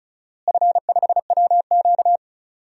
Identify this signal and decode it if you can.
Morse code, heard as a keyed tone, about 36 wpm, 700 Hz. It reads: F5WQ
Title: DXCC-100 Call Signs - 4 Characters - Repeated 3x - 35wpm